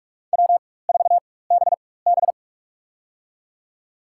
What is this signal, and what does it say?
Morse code, heard as a keyed tone, about 47 wpm, spaced as for 21 wpm, 700 Hz. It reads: W4BB
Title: USA Call Signs - 4 Characters - 4x Character Spacing - 45wpm